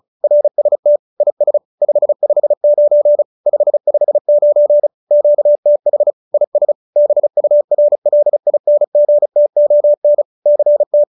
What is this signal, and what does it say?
Morse code, heard as a keyed tone, about 34 wpm, 590 Hz. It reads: RST IS 559 559 QTH IS BURLINGTON CT